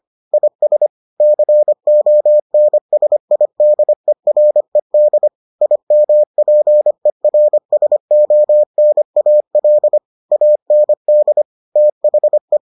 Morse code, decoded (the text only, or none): IS CONSIDERED IMPERSONAL AND THE